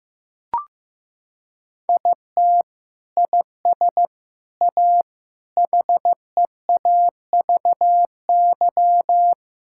Morse code, heard as a keyed tone, about 16 wpm, 700 Hz. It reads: IT IS A HEAVY
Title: Sentences from Top 1000 Words - Review Part 6 - 15wpm